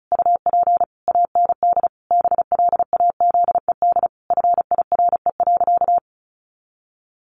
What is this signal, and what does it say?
Morse code, heard as a keyed tone, about 34 wpm, 700 Hz. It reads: UP AND BLAZED FIRE.
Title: A Sound Of Thunder 35wpm